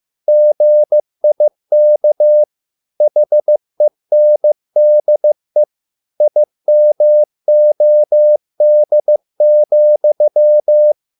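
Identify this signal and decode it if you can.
Morse code, heard as a keyed tone, about 15 wpm, 600 Hz. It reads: GIK HENDE IMOD,
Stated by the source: Morse code at a slow 15 wpm from CW med Gnister.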